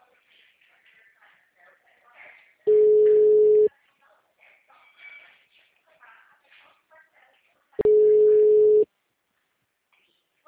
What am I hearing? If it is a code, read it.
Ringback tone (Europe, 425 Hz), with speech.
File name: out- -2005- - - .7429